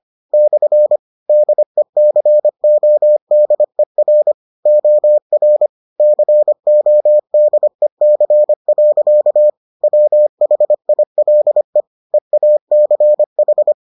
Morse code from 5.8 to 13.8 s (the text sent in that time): CODEC. WHILE EACH